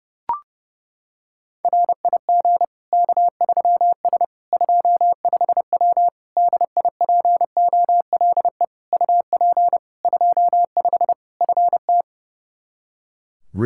Morse code, heard as a keyed tone, about 32 wpm, 700 Hz. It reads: RIG K3S 25W DIPOLE UP 25 FT
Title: K1USN SST QSO Practice 30wpm